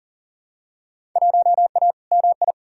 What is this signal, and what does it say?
Morse code, heard as a keyed tone, about 40 wpm, 700 Hz. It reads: 1A MI
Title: Field Day Exchanges - 40wpm